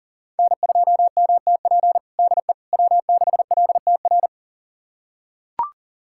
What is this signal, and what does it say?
Morse code, heard as a keyed tone, about 40 wpm, 700 Hz. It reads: N2MTP DE W6LTR
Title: USA Callsign DE Callsign - 5 Characters - 40wpm